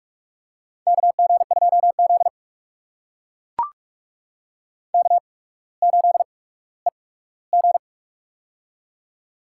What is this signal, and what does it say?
Morse code, heard as a keyed tone, about 43 wpm, spaced as for 19 wpm, 700 Hz. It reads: KG2Z K8EG
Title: USA Call Signs - 4 Characters - 8x Character Spacing - 45wpm